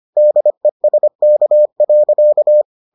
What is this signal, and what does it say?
Morse code, heard as a keyed tone, about 25 wpm, 600 Hz. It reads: DESK.